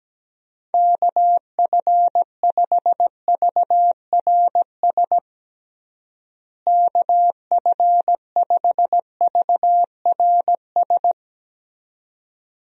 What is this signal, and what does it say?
Morse code, heard as a keyed tone, about 17 wpm, 700 Hz. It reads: KF5VRS KF5VRS
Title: USA Call Signs - 6 Characters - Repeated 2x - 17wpm